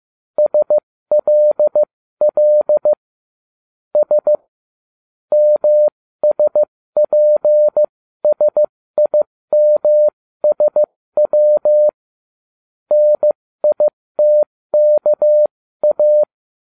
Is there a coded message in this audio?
Yes: Morse code from 0.4 to 16.2 s, heard as a keyed tone, about 15 wpm, spaced as for 13 wpm, 600 Hz. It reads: SLL S MSPSIMSW NITKA